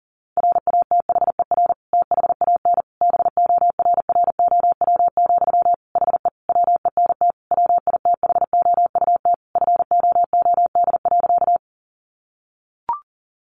Morse code, heard as a keyed tone, about 40 wpm, 700 Hz. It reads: RATHER THAN BORROW, HE WENT WITHOUT FOOD.
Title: Sentences from Top 1500 Words 40wpm